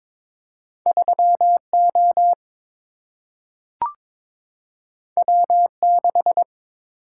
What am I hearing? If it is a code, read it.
Morse code, heard as a keyed tone, about 22 wpm, 700 Hz. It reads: 3 O W 6
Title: Single-Letter Single-Number 22wpm